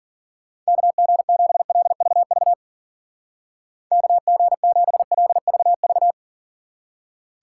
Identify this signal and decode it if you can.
Morse code, heard as a keyed tone, about 50 wpm, 700 Hz. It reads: KG7LVV KG7LVV
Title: Speed Racing - USA 2x3 Call Signs 40wpm